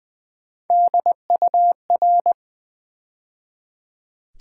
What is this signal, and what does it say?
Morse code, heard as a keyed tone, about 20 wpm, 700 Hz. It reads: DUR
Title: Build A Word - 3 Letters At A Time - 8-Letter Words - 20wpm